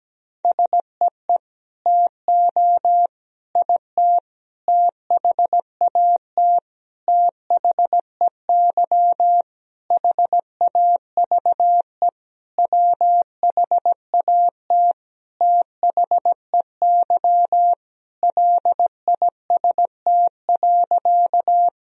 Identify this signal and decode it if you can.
Morse code, heard as a keyed tone, about 17 wpm, 700 Hz. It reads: SEE TO IT THAT THEY HAVE WHAT THEY LIST.
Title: Sentences from Top 1000 Words - Review Part 2 - 17wpm